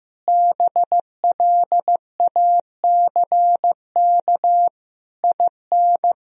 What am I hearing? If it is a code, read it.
Morse code, heard as a keyed tone, about 16 wpm, 700 Hz. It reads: BLACK IN
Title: Sentences from Top 1000 Words - Review Part 2 - 15wpm